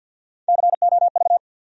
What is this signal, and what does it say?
Morse code, heard as a keyed tone, about 47 wpm, 700 Hz. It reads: COV
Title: USA Call Sign Postfixes - 3 Letters 50wpm